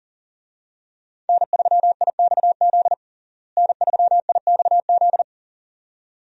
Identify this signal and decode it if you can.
Morse code, heard as a keyed tone, about 40 wpm, 700 Hz. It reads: N3IXZ N3IXZ
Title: USA Call Signs - 5 Characters - Repeated 2x - 40wpm